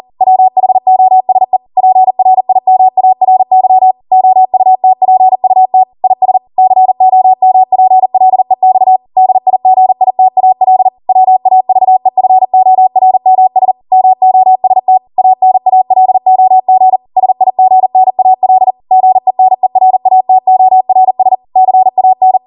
Morse code, heard as a keyed tone, about 40 wpm, 750 Hz. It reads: WHOSE PRIMARY OUTPUT IS COMPLEX DIGITAL WAVEFORMS MOST ANALOG SIGNAL GENERATORS CAN